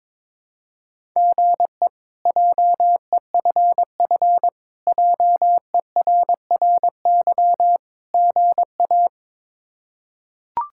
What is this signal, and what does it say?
Morse code, heard as a keyed tone, about 22 wpm, 700 Hz. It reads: GE JEFF JERRY GA
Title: K1USN SST QSO Practice - 22wpm